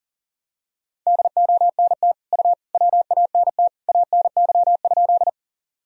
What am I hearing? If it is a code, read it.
Morse code, heard as a keyed tone, about 40 wpm, 700 Hz. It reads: DONT U WANT ANY?
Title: Sentences from Top 300 Words and Abbreviations 40wpm